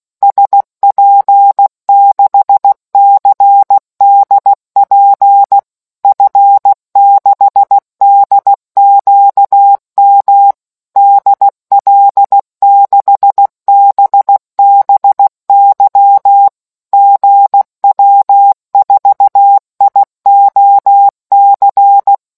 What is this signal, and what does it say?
Morse code, heard as a keyed tone, about 16 wpm, 800 Hz. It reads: SP6CDP F6DQM DL6BBY GW4IOC